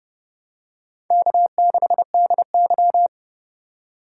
Morse code, heard as a keyed tone, about 28 wpm, 700 Hz. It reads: K6DY